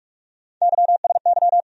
Morse code, heard as a keyed tone, about 47 wpm, 700 Hz. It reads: YSY
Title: USA Call Sign Postfixes - 3 Letters 45wpm